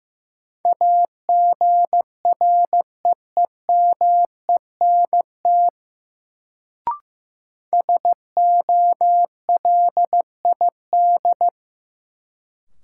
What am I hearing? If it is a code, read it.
Morse code, heard as a keyed tone, about 15 wpm, 700 Hz. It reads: AGREEMENT SOLID
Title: Sentences from Top 1500 Words - New Word Drill - 15wpm